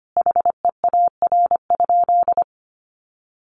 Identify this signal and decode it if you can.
Morse code, heard as a keyed tone, about 25 wpm, 700 Hz. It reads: HEAR?
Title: Sentences from Top 800 Words 25wpm